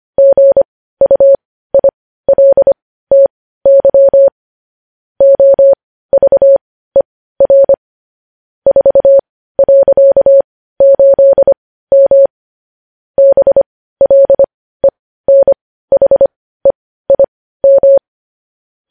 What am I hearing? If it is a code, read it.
Morse code, heard as a keyed tone, about 25 wpm, spaced as for 15 wpm, 550 Hz. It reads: GUILTY OVER 4.8M BLENHEIM
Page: news headlines in morse code at 15 wpm